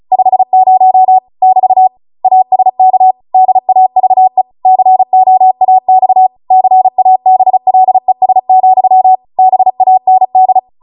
Morse code, heard as a keyed tone, about 36 wpm, 750 Hz. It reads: 50 <BT> ASK DAVE COAX CABLES, BAND